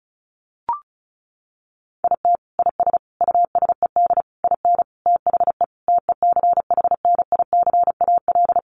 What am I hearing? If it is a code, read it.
Morse code, heard as a keyed tone, about 34 wpm, 700 Hz. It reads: IT IS USED IN THE TECHNICAL
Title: Sentences from Top 1100 Words 35wpm